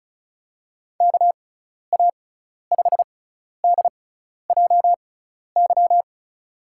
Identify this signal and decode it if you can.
Morse code, heard as a keyed tone, about 36 wpm, spaced as for 12 wpm, 700 Hz. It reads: KA5DJY
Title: USA Call Signs - 6x Character Spacing - 35wpm